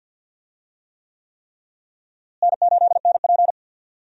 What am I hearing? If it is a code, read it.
Morse code, heard as a keyed tone, about 52 wpm, 700 Hz. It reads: N8NP
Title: USA Call Signs - 4 Characters - Repeated 3x - 50wpm